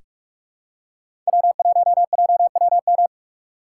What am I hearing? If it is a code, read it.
Morse code, heard as a keyed tone, about 46 wpm, 700 Hz. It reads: W1JWM
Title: USA Call Signs - 6x Character Spacing - 45wpm